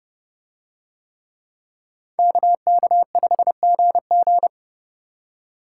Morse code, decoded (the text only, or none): KK5GG